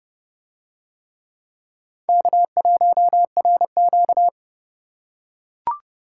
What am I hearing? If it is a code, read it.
Morse code, heard as a keyed tone, about 28 wpm, 700 Hz. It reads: K1RQ